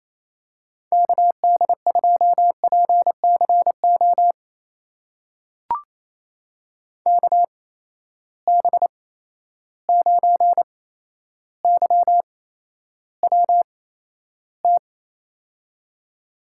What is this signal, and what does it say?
Morse code, heard as a keyed tone, about 28 wpm, spaced as for 12 wpm, 700 Hz. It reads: KD2PCO KB9YWT
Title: USA Call Signs - 6 Characters - 8x Character Spacing - 28wpm